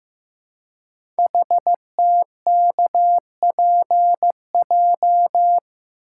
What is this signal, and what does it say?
Morse code, heard as a keyed tone, about 15 wpm, 700 Hz. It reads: HTKPJ
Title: Code Groups - 5 Letter:Numbers - 15wpm